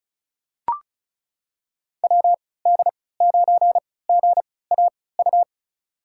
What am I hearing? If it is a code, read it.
Morse code, heard as a keyed tone, about 34 wpm, spaced as for 20 wpm, 700 Hz. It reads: WD9GAU
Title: USA Call Signs - 6 Characters - 3x Character Spacing - 35wpm